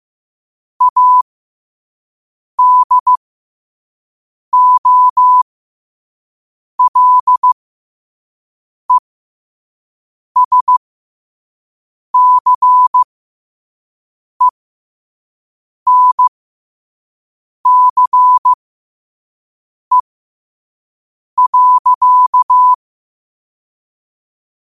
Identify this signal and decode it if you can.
Morse code, heard as a keyed tone, about 15 wpm, spaced as for 5 wpm, 1000 Hz. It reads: ADOLESCENCE.
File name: qotdfiles / 5wpm